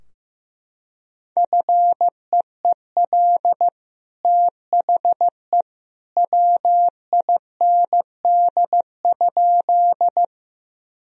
Morse code, heard as a keyed tone, about 16 wpm, 700 Hz. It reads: FEEL THE WIND?
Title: Sentences from Top 1000 Words - Review Part 2 - 15wpm